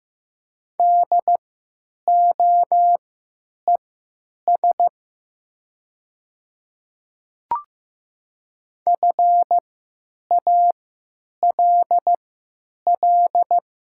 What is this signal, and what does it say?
Morse code, heard as a keyed tone, about 15 wpm, spaced as for 8 wpm, 700 Hz. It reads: DOES FALL
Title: 3x Character Spacing - Top 500 Words - 15wpm